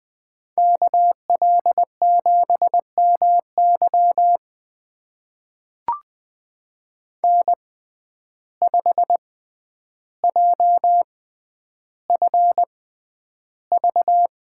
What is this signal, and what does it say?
Morse code, heard as a keyed tone, about 20 wpm, spaced as for 10 wpm, 700 Hz. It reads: KL7MY N5JFV
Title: USA Call Signs - 5 Characters - 6x Character Spacing - 20wpm